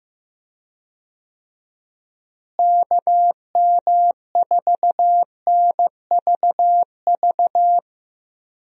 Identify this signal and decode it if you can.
Morse code, heard as a keyed tone, about 14 wpm, 700 Hz. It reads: KM4NVV
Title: USA Call Signs - 6 Characters - Repeated 2x - 15wpm